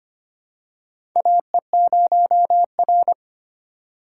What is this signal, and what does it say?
Morse code, heard as a keyed tone, about 25 wpm, 700 Hz. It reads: AE0R
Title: USA Call Signs - 4 Characters - 3x Character Spacing - 25wpm